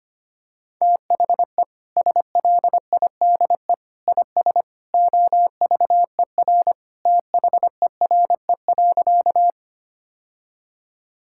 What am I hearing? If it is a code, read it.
Morse code, heard as a keyed tone, about 25 wpm, 700 Hz. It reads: THE SLIDE IS OVER THERE.
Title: Sentences from Top 1600 Words 25wpm